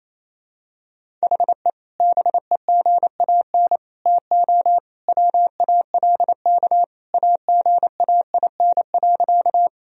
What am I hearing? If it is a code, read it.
Morse code, heard as a keyed tone, about 28 wpm, 700 Hz. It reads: HE BEGAN TO WALK AGAIN.
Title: Sentences from Top 1000 Words - Review Part 1 - 28wpm